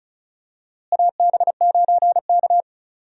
Morse code, heard as a keyed tone, about 34 wpm, 700 Hz. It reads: AB9K
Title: USA Call Signs - 4 Characters - Repeated 2x - 35wpm